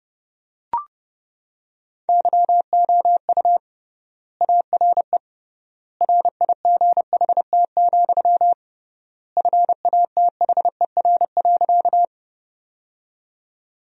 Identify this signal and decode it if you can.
Morse code, heard as a keyed tone, about 32 wpm, 700 Hz. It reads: YOU ARE RIGHT, FATHER.
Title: 3x Word-Spacing - Sentences from Top 300 Words - 30wpm